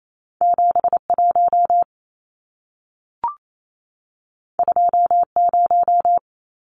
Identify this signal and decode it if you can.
Morse code, heard as a keyed tone, about 28 wpm, 700 Hz. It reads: 71 20
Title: Numbers - 2-Digit 28wpm